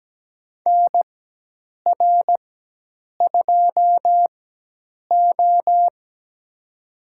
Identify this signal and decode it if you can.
Morse code, heard as a keyed tone, about 17 wpm, spaced as for 8 wpm, 700 Hz. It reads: NR2O
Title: USA Call Signs - 4 Characters - 4x Character Spacing - 17wpm